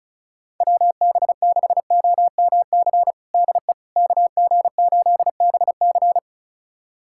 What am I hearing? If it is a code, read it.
Morse code, heard as a keyed tone, about 34 wpm, 690 Hz. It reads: WB6OMC DE KG8BC
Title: USA Callsign DE Callsign 35wpm